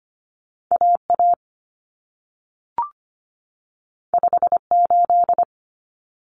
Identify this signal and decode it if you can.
Morse code, heard as a keyed tone, about 25 wpm, 700 Hz. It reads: AA 58
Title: Common Cut Numbers - 2-Digit 25wpm